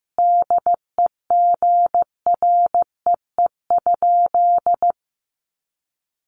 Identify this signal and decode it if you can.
Morse code, heard as a keyed tone, about 14 wpm, 700 Hz. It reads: DEGREE?
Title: Sentences from Top 1500 Words 15wpm